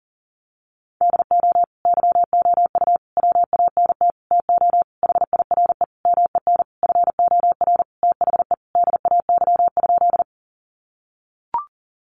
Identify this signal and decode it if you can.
Morse code, heard as a keyed tone, about 37 wpm, 700 Hz. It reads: DO YOU WANT TO HIRE MEN FOR THE DAY?
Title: Sentences from Top 1400 Words 40wpm